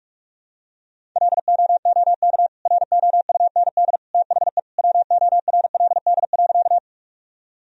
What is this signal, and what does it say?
Morse code, heard as a keyed tone, about 46 wpm, 700 Hz. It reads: LOOK ROUND THE WORLD.
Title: Sentences from Top 1000 Words - Review Part 1 - 45wpm